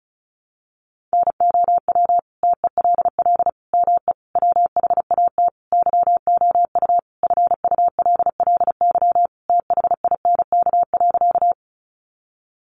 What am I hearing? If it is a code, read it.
Morse code, heard as a keyed tone, about 34 wpm, 700 Hz. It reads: NOW TELL ME WHAT YOU FULLY THINK.
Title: Sentences from Top 1300 Words 35wpm